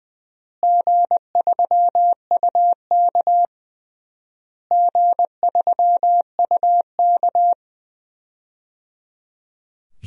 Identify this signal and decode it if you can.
Morse code, heard as a keyed tone, about 20 wpm, 700 Hz. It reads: G3UK G3UK
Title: DXCC-100 Call Signs - 4 Characters - Repeated 3x - 20wpm